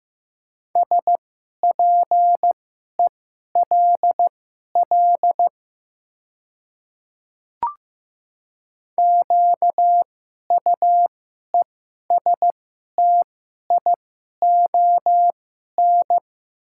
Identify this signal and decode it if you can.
Morse code, heard as a keyed tone, about 15 wpm, spaced as for 11 wpm, 700 Hz. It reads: SPELL QUESTION
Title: 2x Character Spacing - Top 500 Words - 15wpm